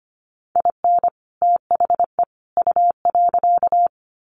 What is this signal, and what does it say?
Morse code, heard as a keyed tone, about 25 wpm, 700 Hz. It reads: IN THE U.
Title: Sets of 3 Words 25wpm